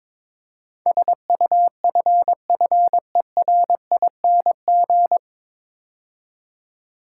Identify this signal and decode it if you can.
Morse code, heard as a keyed tone, about 22 wpm, 700 Hz. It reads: SUFFERING
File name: 9 Letter Words - First 3 Letters Given - 22wpm